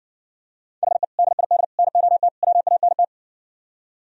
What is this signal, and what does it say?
Morse code, heard as a keyed tone, about 59 wpm, 700 Hz. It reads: HE DID NOT WANT